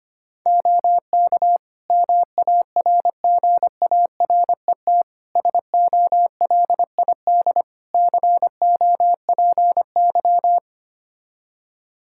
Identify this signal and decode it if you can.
Morse code, heard as a keyed tone, about 25 wpm, 700 Hz. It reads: OK MARGARET SOLID COPY
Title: Practice QSO - Exchange 3 - More Info - 25wpm